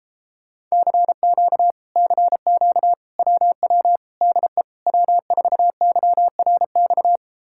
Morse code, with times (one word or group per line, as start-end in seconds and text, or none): 0.7-1.7 CQ
2.0-2.9 CQ
3.2-4.0 WW
4.2-4.6 DE
4.9-7.2 W4YRX